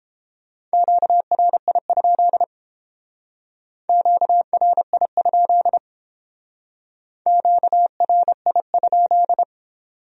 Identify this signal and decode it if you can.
Morse code, heard as a keyed tone, about 31 wpm, 700 Hz. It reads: QRI? QRI? QRI?